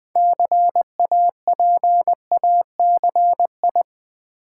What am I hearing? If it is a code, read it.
Morse code, heard as a keyed tone, about 20 wpm, 700 Hz. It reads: CAPACI
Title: Build A Word - 3 Letters At A Time - 8-Letter Words - 20wpm